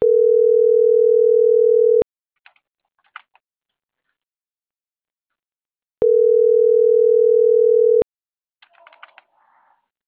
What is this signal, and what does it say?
Ringback tone (North America), 440 + 480 Hz.
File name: CENTRAL MABE / out vanesa.mabe